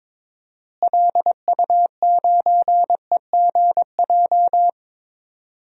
Morse code, heard as a keyed tone, about 22 wpm, 700 Hz. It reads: LU9EGJ
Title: DXCC-100 Call Signs - 6 Characters - Repeated 2x - 22wpm